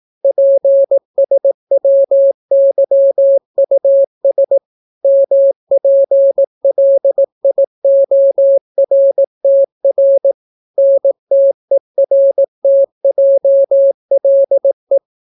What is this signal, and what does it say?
Morse code, heard as a keyed tone, about 18 wpm, 550 Hz. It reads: PSWYUS MPLIORTR NTERTJLE